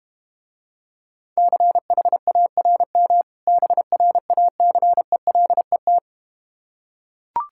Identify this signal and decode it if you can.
Morse code, heard as a keyed tone, about 31 wpm, 690 Hz. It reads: CHARM BRACELET
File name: practice / 25wpm / Difficult Words - Speed-Racing with Hints - 25wpm